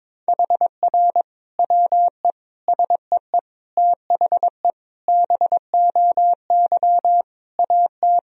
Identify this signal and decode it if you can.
Morse code, heard as a keyed tone, about 21 wpm, 700 Hz. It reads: HR WE SEE THE BOY AT
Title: Sentences from Top 300 Words and Abbreviations 22wpm